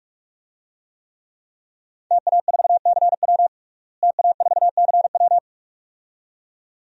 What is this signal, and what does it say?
Morse code, heard as a keyed tone, about 46 wpm, 700 Hz. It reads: TA4CW TA4CW
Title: DXCC-100 Call Signs - 5 Characters - Repeated 2x - 45wpm